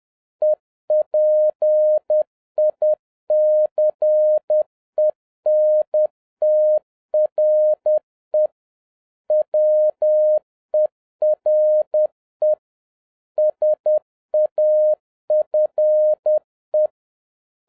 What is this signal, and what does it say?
Morse code, heard as a keyed tone, about 10 wpm, 610 Hz. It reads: EPICENTRE WERE SAFE
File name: 2011-07-mp3 / n36 THE 3.11 2 10wpm